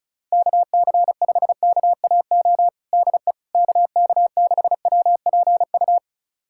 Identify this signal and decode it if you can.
Morse code, heard as a keyed tone, about 34 wpm, 700 Hz. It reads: KC5KAO DE KK6WPU
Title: USA Callsign DE Callsign 35wpm